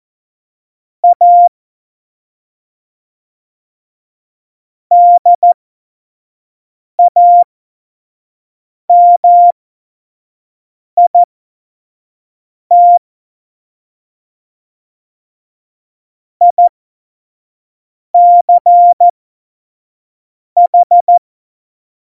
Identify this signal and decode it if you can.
Morse code, heard as a keyed tone, about 14 wpm, spaced as for 5 wpm, 700 Hz. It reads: A DAMIT ICH
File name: Morsekurs / Witze / Witz 27. Februar 2018, 25 F70